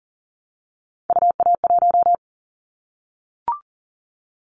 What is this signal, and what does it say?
Morse code, heard as a keyed tone, about 40 wpm, 700 Hz. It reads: UA1
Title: Callsign Prefixes for Top 50 DXCC - 40wpm